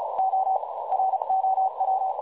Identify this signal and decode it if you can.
Morse code, heard as a keyed tone, about 36 wpm, spaced as for 26 wpm, 760 Hz, with noise. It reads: OROW
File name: dlazaawansowanych / nagraniazpasma / websdr recording 3565.0kHz